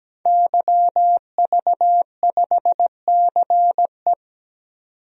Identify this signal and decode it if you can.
Morse code, heard as a keyed tone, about 17 wpm, 700 Hz. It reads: YV5CE
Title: DXCC-100 Call Signs - 5 Characters - 17wpm